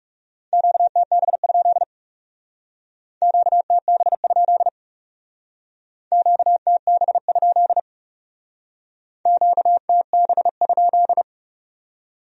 Morse code, heard as a keyed tone, about 38 wpm, 700 Hz. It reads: QTB? QTB? QTB? QTB?